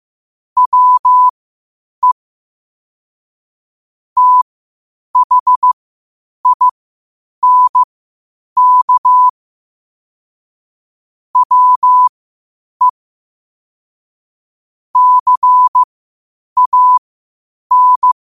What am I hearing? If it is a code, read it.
Morse code, heard as a keyed tone, about 15 wpm, spaced as for 8 wpm, 1000 Hz. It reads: WE THINK WE CAN
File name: qotdfiles / 8wpm